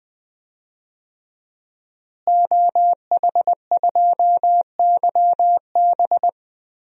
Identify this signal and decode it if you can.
Morse code, heard as a keyed tone, about 21 wpm, 700 Hz. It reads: OH2YB
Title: DXCC-100 Call Signs - 5 Characters - Repeated 2x - 20wpm